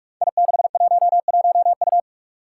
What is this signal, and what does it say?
Morse code, heard as a keyed tone, about 44 wpm, 700 Hz. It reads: I611U
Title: Five Challenging Letter-Number Combinations 45wpm